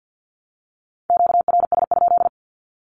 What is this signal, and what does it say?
Morse code, heard as a keyed tone, about 50 wpm, 700 Hz. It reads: QRS?
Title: Q-Codes 50wpm